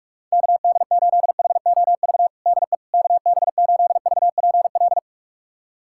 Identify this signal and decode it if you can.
Morse code, heard as a keyed tone, about 46 wpm, 700 Hz. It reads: KD8HOV DE KB8VPL
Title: USA Callsign DE Callsign - 6 Characters - 45wpm